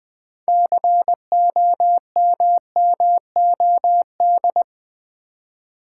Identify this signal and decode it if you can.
Morse code, heard as a keyed tone, about 20 wpm, 700 Hz. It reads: COMMOD
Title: Build A Word - 3 Letters At A Time - 9-Letter Words - 20wpm